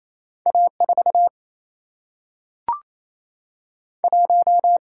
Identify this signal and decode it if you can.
Morse code, heard as a keyed tone, about 28 wpm, 700 Hz. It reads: A4 1
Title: Single-Letter Single-Number 28wpm